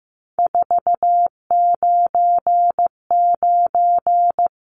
Morse code, heard as a keyed tone, about 15 wpm, 700 Hz. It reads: 499
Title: Numbers - 3-Digit 15wpm